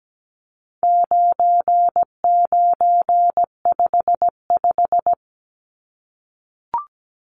Morse code, heard as a keyed tone, about 17 wpm, 700 Hz. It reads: 9955
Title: Numbers - 4-Digit 17wpm